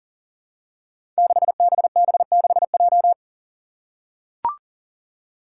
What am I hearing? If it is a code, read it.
Morse code, heard as a keyed tone, about 41 wpm, 700 Hz. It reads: 6BB6J
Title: Five Challenging Letter-Number Combinations 40wpm